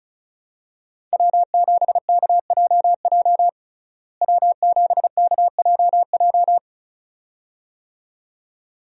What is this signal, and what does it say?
Morse code, heard as a keyed tone, about 34 wpm, 700 Hz. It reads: W7KJJ W7KJJ
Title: USA Call Signs - 5 Characters - Repeated 2x - 35wpm